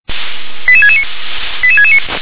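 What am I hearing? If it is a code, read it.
Five-tone selective call (ZVEI), heard as quick runs of five tones, tones of about 70 ms. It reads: 88669 88669
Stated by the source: ZVEI five-tone call 88126 received on an SDR.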